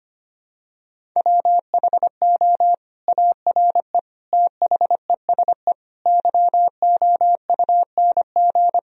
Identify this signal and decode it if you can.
Morse code, heard as a keyed tone, about 25 wpm, 700 Hz. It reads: WHO ARE THESE YOUNG
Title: Sentences from Top 1000 Words - Review Part 1 - 25wpm